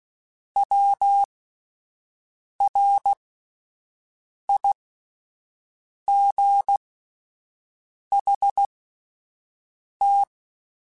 Morse code, heard as a keyed tone, about 16 wpm, spaced as for 5 wpm, 800 Hz. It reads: WRIGHT
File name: Misc / Misc-mp3s / Words @5